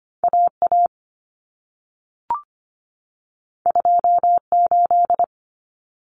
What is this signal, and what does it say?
Morse code, heard as a keyed tone, about 25 wpm, 700 Hz. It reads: AA 28
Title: Common Cut Numbers - 2-Digit 25wpm